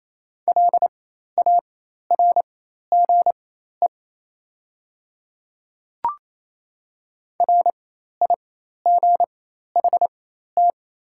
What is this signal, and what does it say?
Morse code, heard as a keyed tone, about 27 wpm, spaced as for 13 wpm, 700 Hz. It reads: LARGE RIGHT
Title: 4x Character Spacing - Top 200 Words - 28wpm